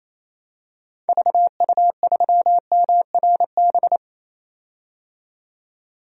Morse code, heard as a keyed tone, about 28 wpm, 700 Hz. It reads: VU3MRB
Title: DXCC-100 Call Signs - 6 Characters - 28wpm